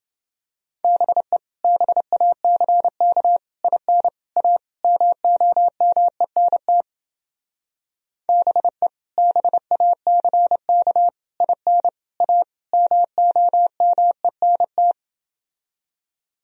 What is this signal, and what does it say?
Morse code, heard as a keyed tone, about 28 wpm, 700 Hz. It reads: BE BACK IN A MOMENT BE BACK IN A MOMENT